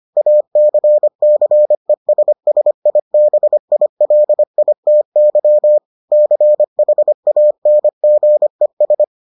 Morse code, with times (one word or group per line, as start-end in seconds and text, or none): 0.2-5.8 ACCESSIBILITY
6.1-9.0 CHANGES